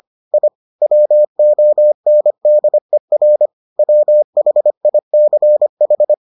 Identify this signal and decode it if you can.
Morse code, heard as a keyed tone, about 25 wpm, 600 Hz. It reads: I WONDER WHICH